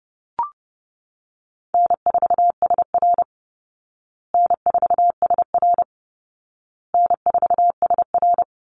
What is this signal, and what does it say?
Morse code, heard as a keyed tone, about 30 wpm, 700 Hz. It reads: N4SR N4SR N4SR
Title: Easy Call Signs Repeated 3x - 30wpm